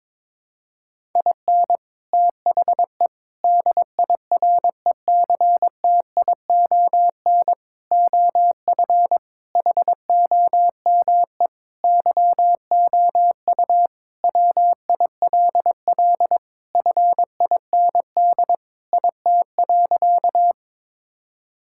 Morse code, heard as a keyed tone, about 22 wpm, 700 Hz. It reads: IN THE DIRECTION OF HOME YOU WILL FIND IT.
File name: practice / 22wpm / Sentences with 4-Word Phrases - 22wpm